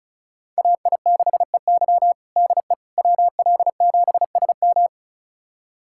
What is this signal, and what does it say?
Morse code, heard as a keyed tone, about 34 wpm, 700 Hz. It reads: AI6EY DE WL7SM
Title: USA Callsign DE Callsign - 5 Characters - 35wpm